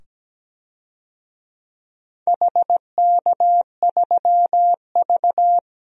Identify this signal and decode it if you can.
Morse code, heard as a keyed tone, about 17 wpm, 700 Hz. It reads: HK3V
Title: DXCC-100 Call Signs - 4 Characters - 17wpm